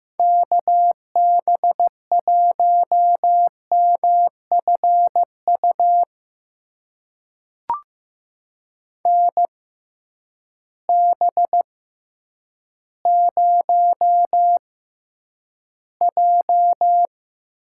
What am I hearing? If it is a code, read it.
Morse code, heard as a keyed tone, about 15 wpm, spaced as for 9 wpm, 700 Hz. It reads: KB1MFU NB0J
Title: USA Call Signs - 6x Character Spacing - 15wpm